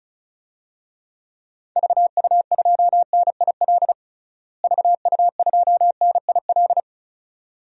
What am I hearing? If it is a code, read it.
Morse code, heard as a keyed tone, about 34 wpm, 700 Hz. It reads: VU2NIL VU2NIL
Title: DXCC-100 Call Signs - 6 Characters - Repeated 2x - 35wpm